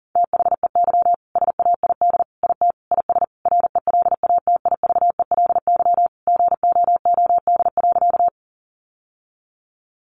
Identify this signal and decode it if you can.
Morse code, heard as a keyed tone, about 40 wpm, 700 Hz. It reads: THEY SAID IT IS RELATIVELY GOOD.
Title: Sentences from Top 1100 Words 40wpm